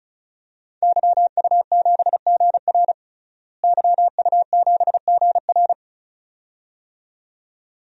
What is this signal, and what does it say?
Morse code, heard as a keyed tone, about 35 wpm, 700 Hz. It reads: YU7GR YU7GR
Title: DXCC-100 Call Signs - 5 Characters - Repeated 2x - 35wpm